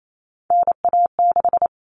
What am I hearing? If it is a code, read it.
Morse code, heard as a keyed tone, about 28 wpm, 700 Hz. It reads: NA6
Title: Common Cut Numbers - 3-Digit 28wpm